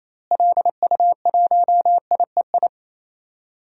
Morse code, heard as a keyed tone, about 28 wpm, 700 Hz. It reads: LU1IEI
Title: DXCC-100 Call Signs - 6 Characters - Repeated 2x - 28wpm